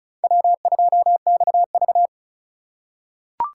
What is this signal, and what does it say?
Morse code, heard as a keyed tone, about 36 wpm, 700 Hz. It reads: W2XV
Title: USA Call Signs - 4 Characters - 4x Character Spacing - 35wpm